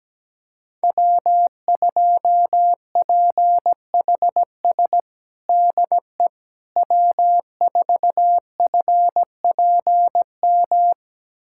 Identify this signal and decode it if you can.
Morse code, heard as a keyed tone, about 17 wpm, 700 Hz. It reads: W2PHS DE W4FPM
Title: USA Callsign DE Callsign - 5 Characters - 17wpm